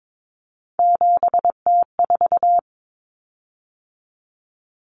Morse code, heard as a keyed tone, about 22 wpm, 700 Hz. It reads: 7T4
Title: Common Cut Numbers - 3-Digit 22wpm